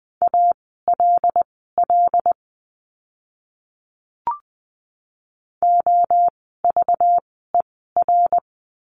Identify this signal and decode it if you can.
Morse code, heard as a keyed tone, about 20 wpm, spaced as for 15 wpm, 700 Hz. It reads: ALL OVER
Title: 230 Lesson 37f - Letter x - Words 20wpm